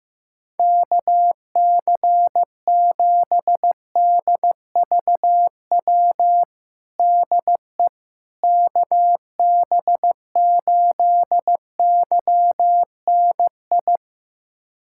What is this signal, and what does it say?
Morse code, heard as a keyed tone, about 15 wpm, 700 Hz. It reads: KC7DVW DE KB8YNI
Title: USA Callsign DE Callsign 15wpm